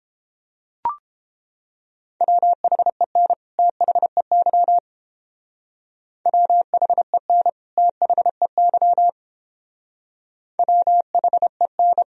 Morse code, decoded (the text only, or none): WHEN THEY WHEN THEY WHEN